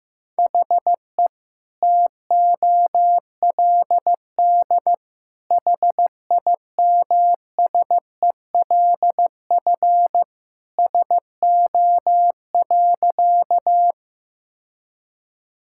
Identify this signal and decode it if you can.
Morse code, heard as a keyed tone, about 15 wpm, 700 Hz. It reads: HE TOLD HIMSELF SO.
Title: Sentences from Top 1600 Words 15wpm